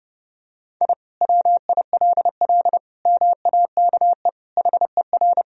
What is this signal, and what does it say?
Morse code, heard as a keyed tone, about 30 wpm, 700 Hz. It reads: I WILL MAKE HER
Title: Sentences from Top 1000 Words - Review Part 1 - 30wpm